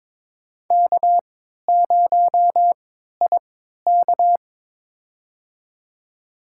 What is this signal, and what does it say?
Morse code, heard as a keyed tone, about 23 wpm, spaced as for 13 wpm, 700 Hz. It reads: K0IK
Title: USA Call Signs - 3x Character Spacing - 22wpm